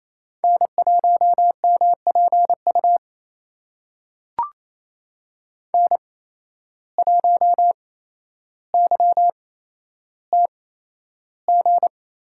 Morse code, heard as a keyed tone, about 28 wpm, spaced as for 12 wpm, 700 Hz. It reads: N1MPU N1YTG
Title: USA Call Signs - 5 Characters - 8x Character Spacing - 28wpm